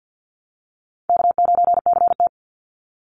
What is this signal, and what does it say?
Morse code, heard as a keyed tone, about 50 wpm, 700 Hz. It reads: K9CT
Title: Call Signs - 50wpm